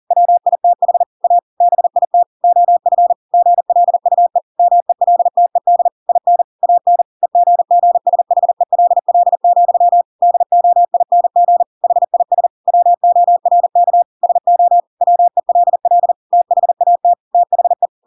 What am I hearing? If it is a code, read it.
Morse code, heard as a keyed tone, about 40 wpm, 700 Hz. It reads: WITH A BIT OF GLUE MELTED IN AN EGGSHELL, DOING HIS WORK SO WELL THAT THE
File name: Pinocchio / Pinocchio ch8 40 40 ik1hge